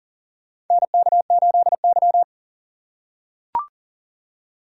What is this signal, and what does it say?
Morse code, heard as a keyed tone, about 40 wpm, 700 Hz. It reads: NK8Y
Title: USA Call Signs - 4 Characters - 3x Character Spacing - 40wpm